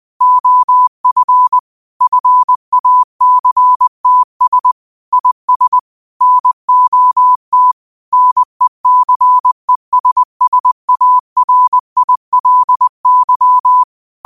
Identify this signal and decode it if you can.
Morse code, heard as a keyed tone, about 20 wpm, 1000 Hz. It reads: OF FACTS IS NOT NECESSARILY
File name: qotdfiles / 20wpm